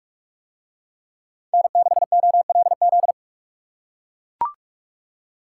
Morse code, heard as a keyed tone, about 41 wpm, 700 Hz. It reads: N6OLZ